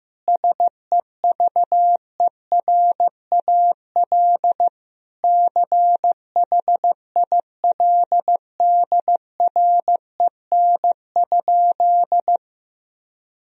Morse code, heard as a keyed tone, about 15 wpm, 700 Hz. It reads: SEVERAL CHILDREN?
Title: Sentences from Top 1000 Words - Review Part 2 - 15wpm